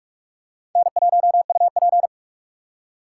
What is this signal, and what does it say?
Morse code, heard as a keyed tone, about 44 wpm, 700 Hz. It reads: N1UP
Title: USA Call Signs - 4 Characters - 6x Character Spacing - 45wpm